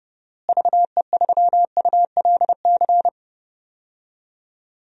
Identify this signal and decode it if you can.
Morse code, heard as a keyed tone, about 30 wpm, 700 Hz. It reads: VE3ULC
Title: DXCC-100 Call Signs - 6 Characters - Repeated 2x - 30wpm